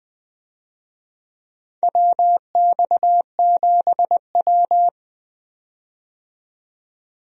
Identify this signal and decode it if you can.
Morse code, heard as a keyed tone, about 20 wpm, 700 Hz. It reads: WX7W
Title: USA Call Signs - 4 Characters - 20wpm